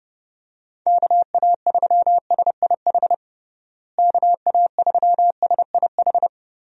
Morse code, heard as a keyed tone, about 30 wpm, 700 Hz. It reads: KA3SIH KA3SIH
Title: USA Call Signs - 6 Characters - Repeated 3x - 30wpm